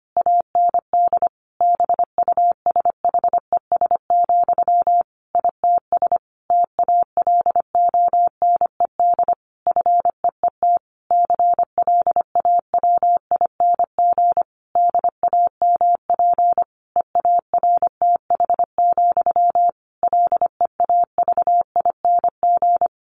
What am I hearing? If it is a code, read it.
Morse code, heard as a keyed tone, about 25 wpm, 700 Hz. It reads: AND BUSHES, ITS TALONED FEET CLAWING DAMP EARTH, LEAVING
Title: A Sound Of Thunder 25wpm